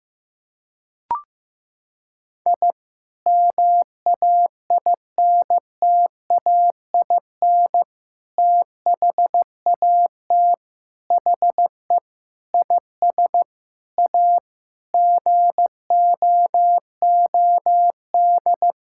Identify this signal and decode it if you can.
Morse code, heard as a keyed tone, about 15 wpm, 700 Hz. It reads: I MAINTAIN THAT HE IS A GOOD
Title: Sentences from Top 1000 Words - Review Part 5 - 15wpm